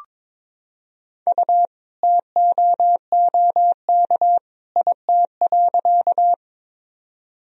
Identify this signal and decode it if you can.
Morse code, heard as a keyed tone, about 22 wpm, 700 Hz. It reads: U TOOK IT.
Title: Sentences from Top 300 Words and Abbreviations 22wpm